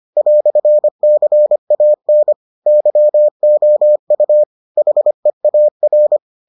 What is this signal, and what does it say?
Morse code, heard as a keyed tone, about 25 wpm, 600 Hz. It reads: "CAN YOU HEAR